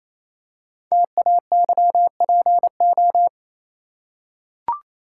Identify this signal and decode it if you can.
Morse code, heard as a keyed tone, about 28 wpm, 700 Hz. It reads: TAYPO
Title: Code Groups - 5 Letters - 28wpm